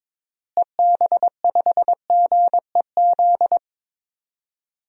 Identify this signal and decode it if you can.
Morse code, heard as a keyed tone, about 22 wpm, 700 Hz. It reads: EB5GEZ
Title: DXCC-100 Call Signs - 6 Characters - Repeated 2x - 22wpm